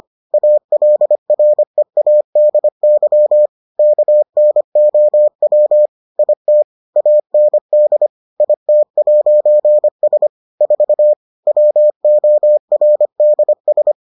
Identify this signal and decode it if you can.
Morse code, heard as a keyed tone, about 25 wpm, 600 Hz. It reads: ALREADY KNOW IT AND IT'S 4 WORDS